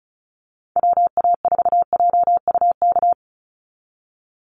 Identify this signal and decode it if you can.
Morse code, heard as a keyed tone, about 34 wpm, 700 Hz. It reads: WA4JUK
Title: Call Signs Repeated 3x - 35wpm